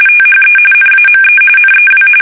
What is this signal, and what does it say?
RTTY, about 45.45 baud. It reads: THE NAME NEW
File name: fsk850 / battleshipnj